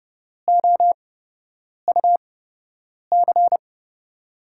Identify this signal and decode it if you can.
Morse code, heard as a keyed tone, about 30 wpm, spaced as for 8 wpm, 700 Hz. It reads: OUC
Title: USA Call Signs - 8x Character Spacing - 30wpm